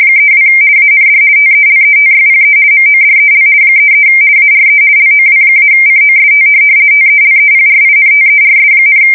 RTTY, about 50 baud. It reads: 32 AND TELETYPE MODEL 33 ARE LOW-COST TELEPRINTERS,
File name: fsk170-50B / teletype-corp